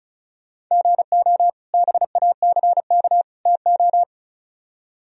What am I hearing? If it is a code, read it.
Morse code, heard as a keyed tone, about 44 wpm, 700 Hz. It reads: GO BACK TO